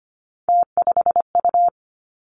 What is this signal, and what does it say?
Morse code, heard as a keyed tone, about 25 wpm, 700 Hz. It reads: T5U
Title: Three Challenging Letter-Number Combinations 25wpm